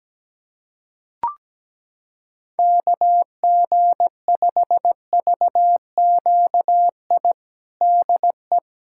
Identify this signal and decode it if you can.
Morse code, heard as a keyed tone, about 17 wpm, 700 Hz. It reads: KG5VQI DE
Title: USA Callsign DE Callsign 17wpm